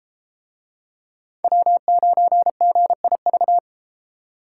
Morse code, heard as a keyed tone, about 31 wpm, 700 Hz. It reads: W9GIV